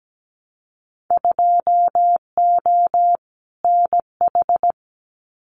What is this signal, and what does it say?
Morse code, heard as a keyed tone, about 17 wpm, 700 Hz. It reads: 2O NH
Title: Winter Field Day Exchanges Encore 17wpm